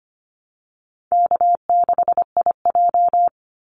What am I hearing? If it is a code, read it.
Morse code, heard as a keyed tone, about 25 wpm, 700 Hz. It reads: K6IJ
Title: Easy Call Signs Repeated 3x - 25wpm